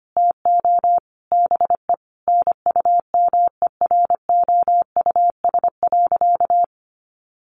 Morse code, heard as a keyed tone, about 25 wpm, 700 Hz. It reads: TO BE NUMEROUS.
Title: Sentences from Top 1400 Words 25wpm